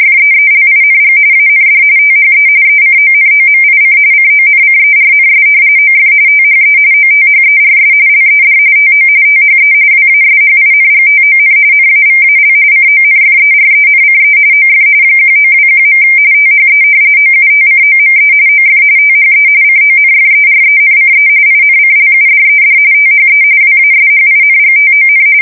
RTTY, about 45.45 baud. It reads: FEELING AND THE ECONOMIC CONCERNS OF THE OTHERS LED TO THE WASHINGTON NAVAL CONFERENCE OF 1921. THE OUTCOME OF THE CONFERENCE INCLUDED THE WASHINGTON
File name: fsk170 / usnavy4